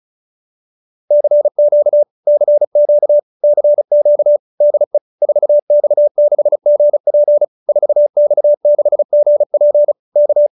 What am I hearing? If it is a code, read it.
Morse code, heard as a keyed tone, about 34 wpm, 590 Hz. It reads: CQ CQ CQ DE 4X6GP 4X6GP K